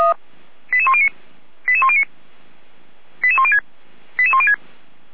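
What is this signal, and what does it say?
Five-tone selective call (ZVEI), heard as quick runs of five tones, tones of about 70 ms. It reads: 88109 88109 88106 88106
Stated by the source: ZVEI five-tone call 88119 received by SDR.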